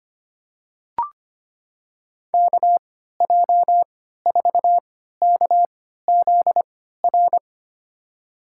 Morse code, heard as a keyed tone, about 25 wpm, spaced as for 14 wpm, 700 Hz. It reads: KJ4KZR
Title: USA Call Signs - 6 Characters - 3x Character Spacing - 25wpm